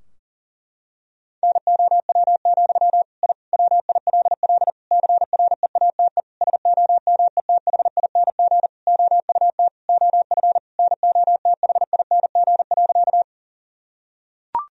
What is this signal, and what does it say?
Morse code, heard as a keyed tone, about 40 wpm, 700 Hz. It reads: NOW, I WILL CREATE SOMETHING OUT OF NOTHING.
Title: Sentences from Top 1000 Words - Review Part 3 - 40wpm